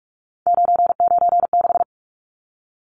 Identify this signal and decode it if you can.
Morse code, heard as a keyed tone, about 44 wpm, 700 Hz. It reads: 996
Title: Numbers - 3-Digit 45wpm